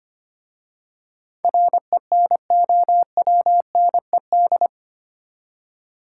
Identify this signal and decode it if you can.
Morse code, heard as a keyed tone, about 25 wpm, 700 Hz. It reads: RENOWNED
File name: 8 Letter Words - First 3 Letters Given - 25wpm